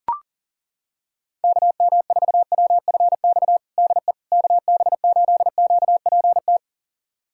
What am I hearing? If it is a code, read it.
Morse code, heard as a keyed tone, about 40 wpm, 700 Hz. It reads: KM4WFX DE KB8QPT
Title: USA Callsign DE Callsign - 6 Characters - 40wpm